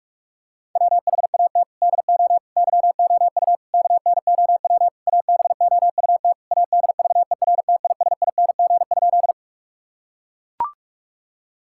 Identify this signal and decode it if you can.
Morse code, heard as a keyed tone, about 41 wpm, 700 Hz. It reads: WHAT DO YOU KNOW ABOUT ADVERTISING?
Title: Sentences from Top 1000 Words - Review Part 6 - 45wpm